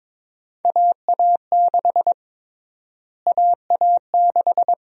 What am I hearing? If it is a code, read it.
Morse code, heard as a keyed tone, about 22 wpm, 700 Hz. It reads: AA6 AA6
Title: USA Call Sign Prefixes - Repeated 2x - 22wpm